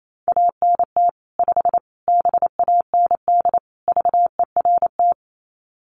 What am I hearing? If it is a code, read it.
Morse code, heard as a keyed tone, about 28 wpm, 700 Hz. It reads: ANT 5 BAND VERT
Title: Common QSO Phrases 28wpm